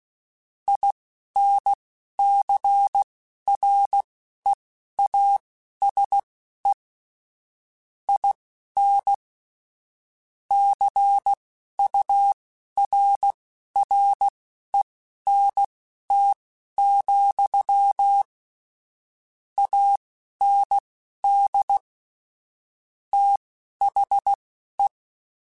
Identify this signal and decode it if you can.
Morse code, heard as a keyed tone, about 16 wpm, spaced as for 11 wpm, 800 Hz. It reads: INCREASE IN CURRENT, AND THE
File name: Technical Subjects / Technical Subjects-mp3s / XL @10